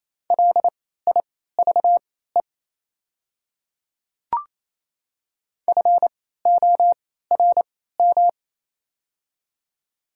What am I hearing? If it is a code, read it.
Morse code, heard as a keyed tone, about 27 wpm, spaced as for 16 wpm, 690 Hz. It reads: LIVE FORM
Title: 3x Character Spacing - Top 500 Words - 28wpm